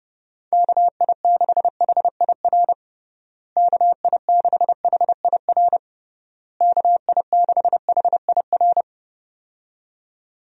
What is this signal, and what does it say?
Morse code, heard as a keyed tone, about 32 wpm, 700 Hz. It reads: KI6HIR KI6HIR KI6HIR
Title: USA Call Signs - 6 Characters - Repeated 3x - 30wpm